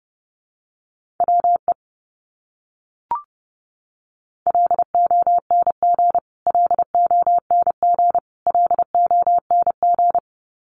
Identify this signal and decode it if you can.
Morse code, heard as a keyed tone, about 30 wpm, 700 Hz. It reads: WE LONG LONG LONG
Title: Top 200 Words Repeated 30wpm